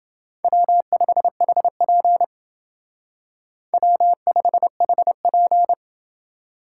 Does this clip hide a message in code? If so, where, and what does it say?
Yes: Morse code from 0.4 to 5.7 s, heard as a keyed tone, about 28 wpm, 700 Hz. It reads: W5HP W5HP